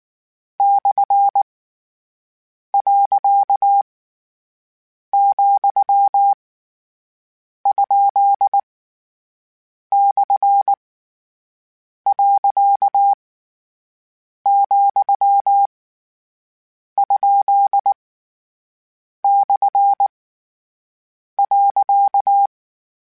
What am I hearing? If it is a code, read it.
Morse code, heard as a keyed tone, about 19 wpm, 800 Hz. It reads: / . , ? / . , ? / .
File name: Misc / Misc-mp3s / Punct @18